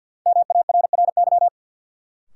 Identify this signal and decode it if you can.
Morse code, heard as a keyed tone, about 53 wpm, 700 Hz. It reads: MARRY
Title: Sentences from Top 1200 Words - New Word Drill Alt - 50wpm